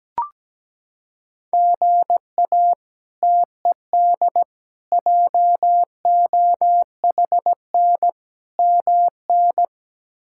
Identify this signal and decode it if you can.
Morse code, heard as a keyed tone, about 17 wpm, 700 Hz. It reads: GA TED JOHN MN
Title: K1USN SST QSO Practice - 17wpm